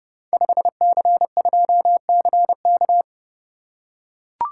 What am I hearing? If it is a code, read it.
Morse code, heard as a keyed tone, about 30 wpm, 700 Hz. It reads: 5C2CK
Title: Code Groups - 5 Letter:Numbers - 30wpm